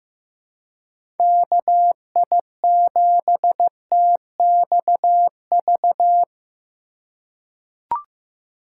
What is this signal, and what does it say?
Morse code, heard as a keyed tone, about 15 wpm, 700 Hz. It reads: KI7TXV
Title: USA Call Signs - 4x Character Spacing - 15wpm